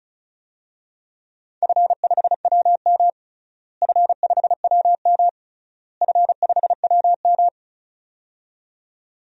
Morse code, heard as a keyed tone, about 36 wpm, 700 Hz. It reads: F5WM F5WM F5WM
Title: DXCC-100 Call Signs - 4 Characters - Repeated 3x - 35wpm